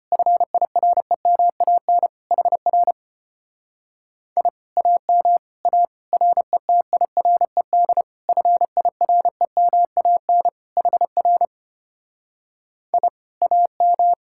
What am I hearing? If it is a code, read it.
Morse code, heard as a keyed tone, about 33 wpm, 700 Hz. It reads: FIREMAN HR I AM A RETIRED FIREMAN HR I AM